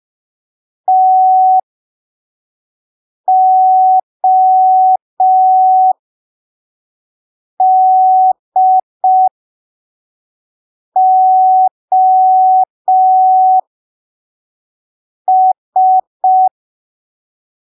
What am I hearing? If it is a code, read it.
Morse code, heard as a keyed tone, about 5 wpm, spaced as for 3 wpm, 750 Hz. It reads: TODOS
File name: arquivos / texto / texto 25 05ppm